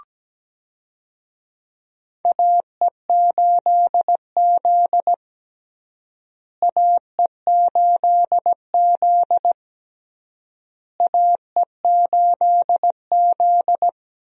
Morse code, heard as a keyed tone, about 17 wpm, 700 Hz. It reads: AE8Z AE8Z AE8Z
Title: USA Call Signs - 4 Characters - Repeated 3x - 17wpm